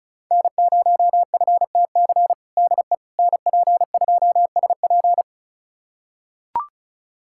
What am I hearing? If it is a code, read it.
Morse code, heard as a keyed tone, about 36 wpm, 700 Hz. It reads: N0FTC DE NP2SP
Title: USA Callsign DE Callsign - 5 Characters - 35wpm